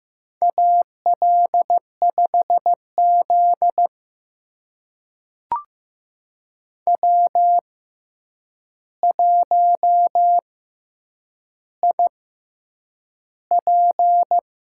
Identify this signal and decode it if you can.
Morse code, heard as a keyed tone, about 15 wpm, spaced as for 8 wpm, 700 Hz. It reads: AL5Z W1IP
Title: USA Call Signs - 4 Characters - 6x Character Spacing - 15wpm